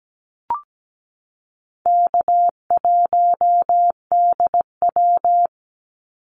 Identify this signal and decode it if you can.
Morse code, heard as a keyed tone, about 17 wpm, 700 Hz. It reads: K1DW
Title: Easy Call Signs - 17wpm